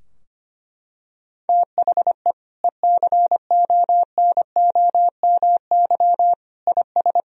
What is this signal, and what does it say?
Morse code, heard as a keyed tone, about 25 wpm, 700 Hz. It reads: THE ECONOMY IS
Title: Sentences from Top 1000 Words - Review Part 3 - 25wpm